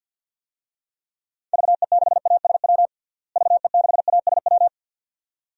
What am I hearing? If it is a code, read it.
Morse code, heard as a keyed tone, about 47 wpm, 700 Hz. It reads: VE6ASW VE6ASW
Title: DXCC-100 Call Signs - 6 Characters - Repeated 2x - 50wpm